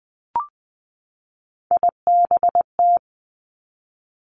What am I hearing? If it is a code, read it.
Morse code, heard as a keyed tone, about 20 wpm, 700 Hz. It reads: IBT
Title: Three Challenging Letter-Number Combinations 20wpm